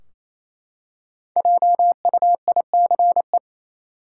Morse code, heard as a keyed tone, about 28 wpm, 700 Hz. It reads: JUICE
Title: Five Letter Words - 28wpm